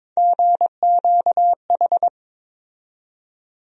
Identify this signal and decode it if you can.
Morse code, heard as a keyed tone, about 22 wpm, 700 Hz. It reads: GQH
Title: USA Call Sign Postfixes - 3 Letters 22wpm